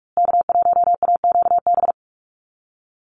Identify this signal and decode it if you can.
Morse code, heard as a keyed tone, about 44 wpm, 700 Hz. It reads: K1AQB
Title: USA Call Signs Repeated 3x 45wpm